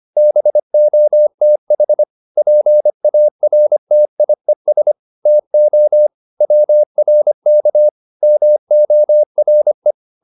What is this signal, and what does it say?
Morse code, heard as a keyed tone, about 25 wpm, 600 Hz. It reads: BOTH PARTIES TO WRK MORE
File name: SS220 25